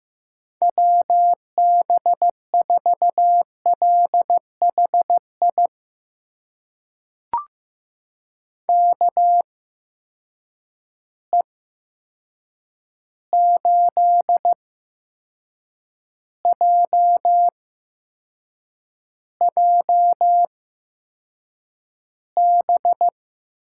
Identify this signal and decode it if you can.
Morse code, heard as a keyed tone, about 15 wpm, spaced as for 6 wpm, 700 Hz. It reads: WB4LHI KE8JJB
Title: USA Call Signs - 6 Characters - 8x Character Spacing - 15wpm